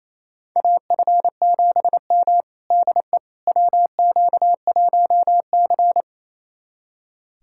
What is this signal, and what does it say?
Morse code, heard as a keyed tone, about 28 wpm, 700 Hz. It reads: AF7M DE WQ1C
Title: USA Callsign DE Callsign - 4 Characters - 28wpm